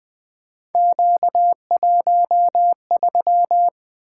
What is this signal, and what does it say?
Morse code, heard as a keyed tone, about 20 wpm, 700 Hz. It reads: Q13
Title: Three Letter-Number Combinations - 20wpm